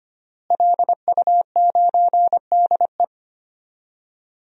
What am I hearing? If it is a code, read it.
Morse code, heard as a keyed tone, about 25 wpm, 700 Hz. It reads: LU9DE
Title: DXCC-100 Call Signs - 5 Characters - Repeated 3x - 25wpm